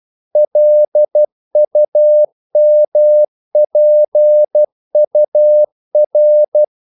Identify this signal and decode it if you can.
Morse code, heard as a keyed tone, about 12 wpm, 600 Hz. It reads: LUMPUR
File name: CW / Practice / NationalCapitals12